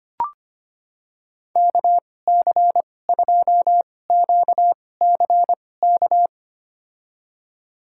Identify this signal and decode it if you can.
Morse code, heard as a keyed tone, about 25 wpm, spaced as for 18 wpm, 700 Hz. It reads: KC2QCK
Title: USA Call Signs - 2x Character Spacing - 25wpm